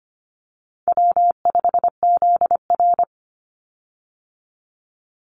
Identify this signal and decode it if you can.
Morse code, heard as a keyed tone, about 25 wpm, 700 Hz. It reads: W5ZR
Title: Easy Call Signs Repeated 2x - 25wpm